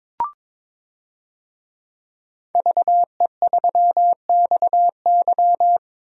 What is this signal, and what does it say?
Morse code, heard as a keyed tone, about 22 wpm, 700 Hz. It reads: VE3XY
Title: DXCC-100 Call Signs - 5 Characters - Repeated 3x - 22wpm